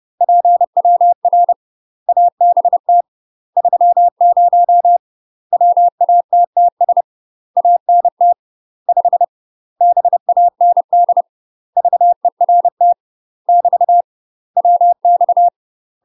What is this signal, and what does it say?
Morse code, heard as a keyed tone, about 30 wpm, 700 Hz. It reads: PWR ABT 30 WATTS ANT 5 BAND VERT <BT> WX
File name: UK MASTER QSOs TRAINING / 30WPM / UK MASTER QSOs 0036